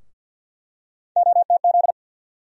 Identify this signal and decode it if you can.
Morse code, heard as a keyed tone, about 50 wpm, 700 Hz. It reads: OTZ
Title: USA Call Sign Postfixes - 3 Letters 50wpm